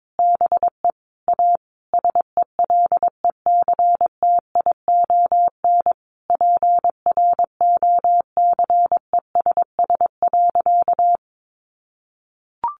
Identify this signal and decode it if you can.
Morse code, heard as a keyed tone, about 22 wpm, 700 Hz. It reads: BE A SELECTION PROCESS.
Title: Sentences from Top 1400 Words 22wpm